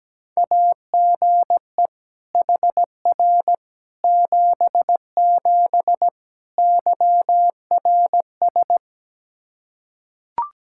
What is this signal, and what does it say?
Morse code, heard as a keyed tone, about 17 wpm, 700 Hz. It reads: AGE HR 77 YRS
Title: Practice QSO - Exchange 3 - More Info - 17wpm